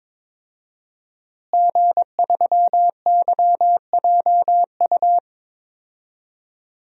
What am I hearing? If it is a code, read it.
Morse code, heard as a keyed tone, about 23 wpm, 700 Hz. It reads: G3YJU
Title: DXCC-100 Call Signs - 5 Characters - Repeated 3x - 22wpm